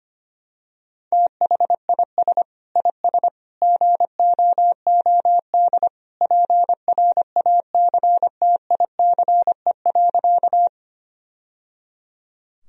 Morse code, heard as a keyed tone, about 25 wpm, 700 Hz. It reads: THIS IS GOOD PRACTICE.
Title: Sentences from Top 1000 Words - Review Part 4 - 25wpm